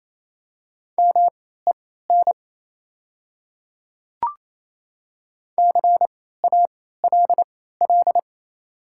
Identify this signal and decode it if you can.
Morse code, heard as a keyed tone, about 28 wpm, spaced as for 16 wpm, 700 Hz. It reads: MEN CALL
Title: 3x Character Spacing - Top 300 Words - 28wpm